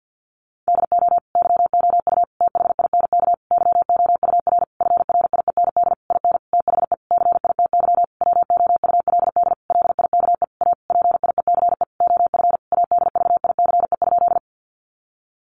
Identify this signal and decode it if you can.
Morse code, heard as a keyed tone, about 48 wpm, 700 Hz. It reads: DO YOU THINK YOUR FRIEND IN THE CITY WOULD LIKE A PIECE OF ADVICE?
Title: Sentences from Top 1100 Words 50wpm